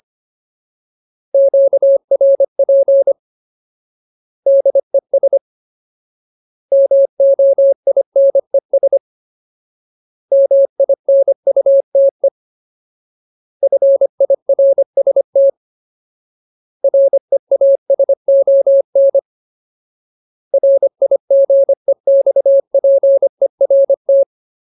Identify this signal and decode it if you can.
Morse code, heard as a keyed tone, about 25 wpm, 550 Hz. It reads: QRP DES MOINES MINUTE FIRST REASON RIGEXPERT
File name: Ultimate Word List / 25WPM Ultimate